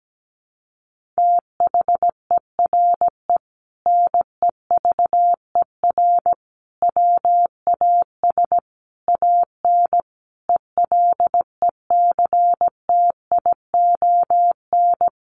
Morse code, heard as a keyed tone, about 17 wpm, 700 Hz. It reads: THERE NEVER WAS AN ELECTION
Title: Sentences from Top 1500 Words 17wpm